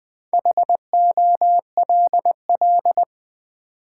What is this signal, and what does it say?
Morse code, heard as a keyed tone, about 20 wpm, 700 Hz. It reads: HOLL
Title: Build A Word - 2 Letters At A Time - 6-Letter Words - 20wpm